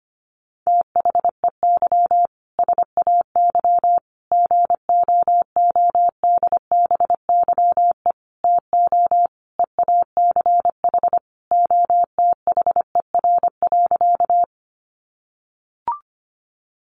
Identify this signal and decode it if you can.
Morse code, heard as a keyed tone, about 25 wpm, 700 Hz. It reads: THEY SAY GOODBYE TO EACH OTHER.
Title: Additional Easy Phrases 25wpm